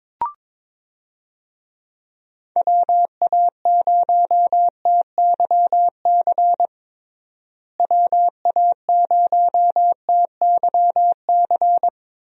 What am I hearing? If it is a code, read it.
Morse code, heard as a keyed tone, about 22 wpm, 700 Hz. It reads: WA0TYC WA0TYC
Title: USA Call Signs - 6 Characters - Repeated 3x - 22wpm